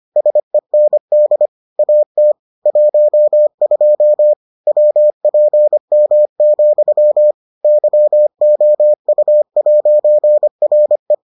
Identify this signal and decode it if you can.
Morse code, heard as a keyed tone, about 25 wpm, 600 Hz. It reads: SEND AT 12 WPM, YOU'RE